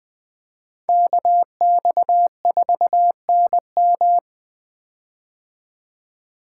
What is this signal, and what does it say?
Morse code, heard as a keyed tone, about 20 wpm, 700 Hz. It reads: KX4NM
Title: K1USN SST QSO Practice - 20wpm